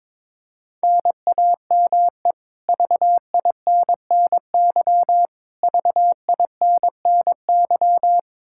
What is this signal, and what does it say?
Morse code, heard as a keyed tone, about 22 wpm, 700 Hz. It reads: NAME VINNY VINNY
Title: K1USN SST QSO Practice 22wpm